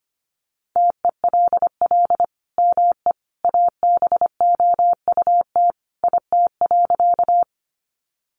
Morse code, heard as a keyed tone, about 25 wpm, 700 Hz. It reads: TELL ME ABOUT IT.
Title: Sentences from Top 200 Words 25wpm